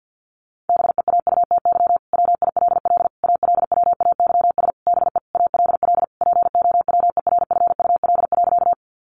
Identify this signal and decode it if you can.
Morse code, heard as a keyed tone, about 50 wpm, 700 Hz. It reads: BEAUTY WILL ALWAYS BE ALL POWERFUL.
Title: Sentences from Top 1300 Words 50wpm